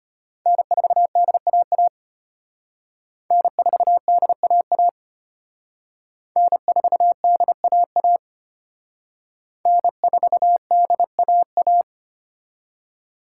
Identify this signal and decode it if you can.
Morse code, heard as a keyed tone, about 31 wpm, 700 Hz. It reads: N4DAA N4DAA N4DAA N4DAA
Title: Speed Racing - USA Call Signs 25wpm